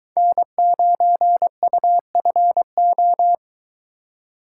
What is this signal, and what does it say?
Morse code, heard as a keyed tone, about 23 wpm, 700 Hz. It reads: N9UFO